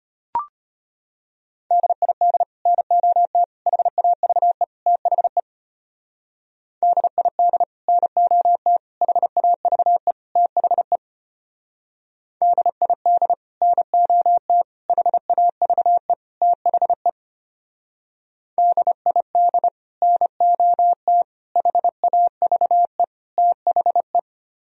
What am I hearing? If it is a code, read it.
Morse code, heard as a keyed tone, about 32 wpm, 700 Hz. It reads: DID NOT HAVE THE DID NOT HAVE THE DID NOT HAVE THE DID NOT HAVE THE